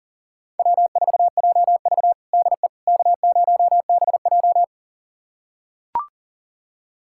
Morse code, heard as a keyed tone, about 40 wpm, 700 Hz. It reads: W4JV DE K0BJ
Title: USA Callsign DE Callsign - 4 Characters - 40wpm